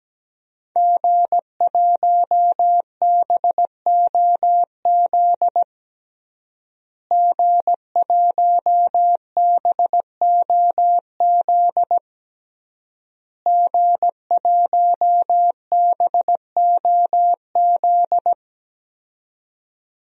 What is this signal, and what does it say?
Morse code, heard as a keyed tone, about 17 wpm, 700 Hz. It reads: G1BOZ G1BOZ G1BOZ
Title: DXCC-100 Call Signs - 5 Characters - Repeated 3x - 17wpm